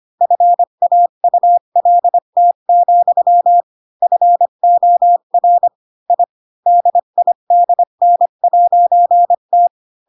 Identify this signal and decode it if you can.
Morse code, heard as a keyed tone, about 25 wpm, 700 Hz. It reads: FAULT, FOR I DIDN'T
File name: Pinocchio / Pinocchio ch7 20 25 ik1hge